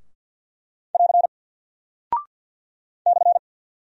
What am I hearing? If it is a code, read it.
Morse code, heard as a keyed tone, about 48 wpm, 700 Hz. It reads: <AR> /
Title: Single Character (Letter:Number:Punctuation:Prosign) Alt 50wpm